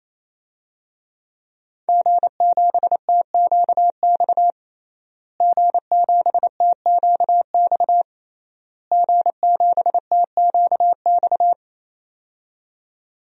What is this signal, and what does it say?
Morse code, heard as a keyed tone, about 28 wpm, 700 Hz. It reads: G7TQX G7TQX G7TQX
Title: DXCC-100 Call Signs - 5 Characters - Repeated 3x - 28wpm